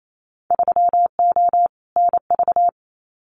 Morse code, heard as a keyed tone, about 28 wpm, 700 Hz. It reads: 3O NV
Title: Winter Field Day Exchanges Encore 28wpm